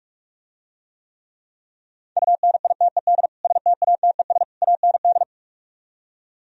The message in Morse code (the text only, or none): UNITED STATES AND